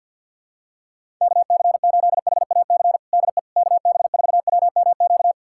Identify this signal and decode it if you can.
Morse code, heard as a keyed tone, about 51 wpm, 700 Hz. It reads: KC8HAC DE KB4WMQ
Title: USA Callsign DE Callsign - 6 Characters - 50wpm